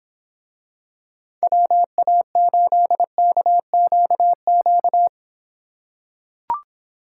Morse code, heard as a keyed tone, about 26 wpm, 700 Hz. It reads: WA8KQQ